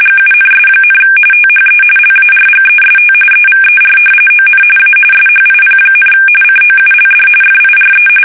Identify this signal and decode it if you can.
RTTY, about 45.45 baud. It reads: GUNS, THE LARGEST EVER ON A SHIP, COULD FIRE
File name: fsk850 / battleshipnj